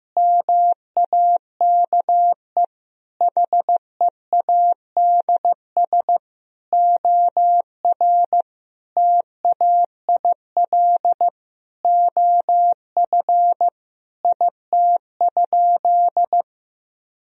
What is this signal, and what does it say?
Morse code, heard as a keyed tone, about 15 wpm, 700 Hz. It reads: MAKE HEADS OR TAIL OF IT?
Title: Sentences from Top 1000 Words - Review Part 3 - 15wpm